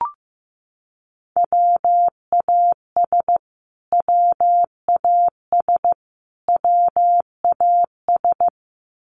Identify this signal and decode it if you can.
Morse code, heard as a keyed tone, about 15 wpm, 700 Hz. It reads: WAS WAS WAS
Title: Top 500 Words Repeated 15wpm